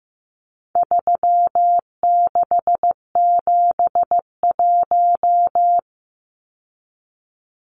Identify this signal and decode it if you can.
Morse code, heard as a keyed tone, about 15 wpm, 700 Hz. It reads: 3671
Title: Numbers - 4-Digit 15wpm